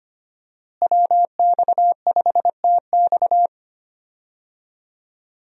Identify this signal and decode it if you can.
Morse code, heard as a keyed tone, about 25 wpm, 700 Hz. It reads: WX5TX
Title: USA Call Signs - 5 Characters - Repeated 3x - 25wpm